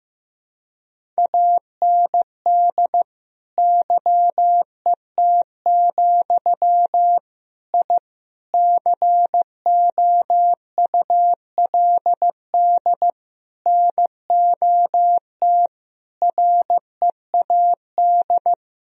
Morse code, heard as a keyed tone, about 15 wpm, 700 Hz. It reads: AND YET, I COULD NOT READ
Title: Sentences from Top 1000 Words - Review Part 1 - 15wpm